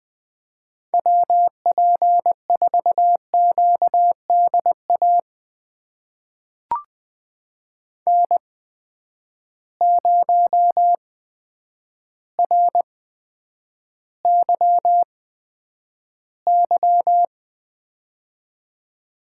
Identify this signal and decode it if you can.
Morse code, heard as a keyed tone, about 20 wpm, spaced as for 9 wpm, 700 Hz. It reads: WP4QDA N0RYY
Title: USA Call Signs - 8x Character Spacing - 20wpm